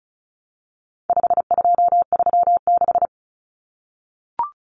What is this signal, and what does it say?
Morse code, heard as a keyed tone, about 34 wpm, 700 Hz. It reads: 5236
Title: Numbers - 4-Digit 35wpm